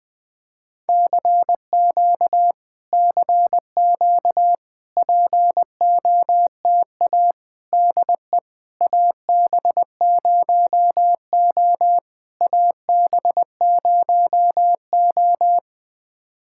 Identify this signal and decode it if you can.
Morse code, heard as a keyed tone, about 20 wpm, 700 Hz. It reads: CQ CQ POTA DE AB0O AB0O
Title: POTA QSO Practice - 20wpm